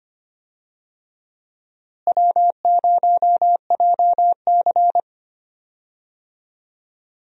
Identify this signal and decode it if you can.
Morse code, heard as a keyed tone, about 25 wpm, 700 Hz. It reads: W0JC
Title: USA Call Signs - 4 Characters - Repeated 3x - 25wpm